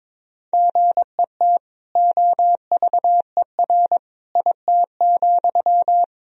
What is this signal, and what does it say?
Morse code, heard as a keyed tone, about 22 wpm, 700 Hz. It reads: GET OVER IT,
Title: Sentences from Top 1000 Words - Review Part 2 - 22wpm